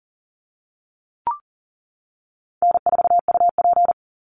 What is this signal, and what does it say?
Morse code, heard as a keyed tone, about 40 wpm, 700 Hz. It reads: N4UP
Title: Easy Call Signs - 40wpm